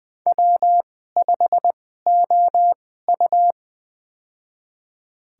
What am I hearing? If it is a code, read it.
Morse code, heard as a keyed tone, about 20 wpm, spaced as for 14 wpm, 700 Hz. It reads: W5OU
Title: USA Call Signs - 4 Characters - 2x Character Spacing - 20wpm